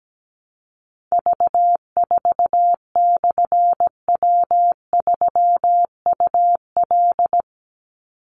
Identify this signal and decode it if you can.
Morse code, heard as a keyed tone, about 17 wpm, 700 Hz. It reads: V4/W3UL
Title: Call Signs - 17wpm